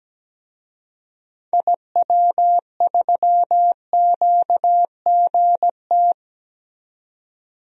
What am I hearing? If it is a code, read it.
Morse code, heard as a keyed tone, about 17 wpm, 700 Hz. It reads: IW3QGT
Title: DXCC-100 Call Signs - 6 Characters - 17wpm